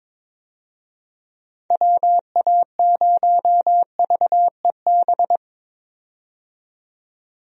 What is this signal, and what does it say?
Morse code, heard as a keyed tone, about 22 wpm, 700 Hz. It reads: WA0VEB
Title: USA Call Signs - 6 Characters - 22wpm